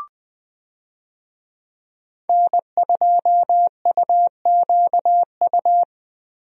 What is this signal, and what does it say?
Morse code, heard as a keyed tone, about 20 wpm, 700 Hz. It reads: N2UQU
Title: USA Call Signs - 5 Characters - Repeated 2x - 20wpm